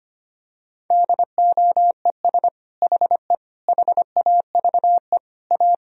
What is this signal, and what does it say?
Morse code, heard as a keyed tone, about 25 wpm, 700 Hz. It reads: DOES HE HAVE A
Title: Sentences from Top 1000 Words - Review Part 2 - 25wpm